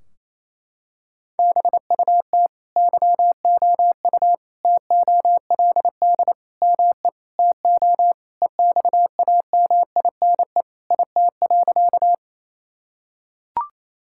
Morse code, heard as a keyed tone, about 28 wpm, 700 Hz. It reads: BUT YOU TOLD ME TO EXAMINE IT.
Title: Sentences from Top 1600 Words 28wpm